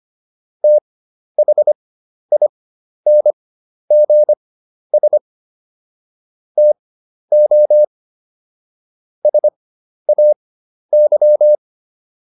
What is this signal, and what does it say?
Morse code, heard as a keyed tone, about 25 wpm, spaced as for 11 wpm, 600 Hz. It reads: THINGS TO SAY